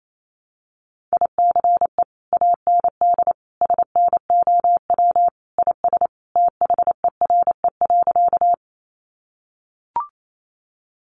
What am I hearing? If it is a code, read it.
Morse code, heard as a keyed tone, about 28 wpm, 700 Hz. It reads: ICE AND SNOW IS THERE.
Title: Sentences from Top 1000 Words 28wpm